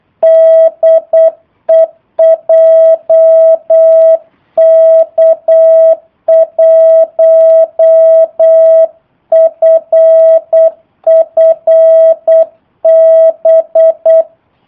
Morse code, heard as a keyed tone, about 8 wpm, 650 Hz. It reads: DE JK1FFB
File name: jh8cbh / 2019cw / sp03- w